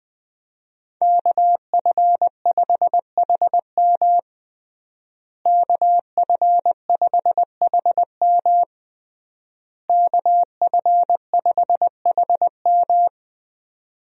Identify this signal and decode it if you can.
Morse code, heard as a keyed tone, about 20 wpm, 700 Hz. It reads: KF5HM KF5HM KF5HM
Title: USA Call Signs - 5 Characters - Repeated 3x - 20wpm